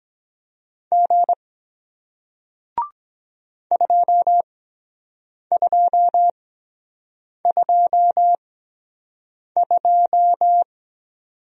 Morse code, heard as a keyed tone, about 22 wpm, spaced as for 7 wpm, 700 Hz. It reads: G 2222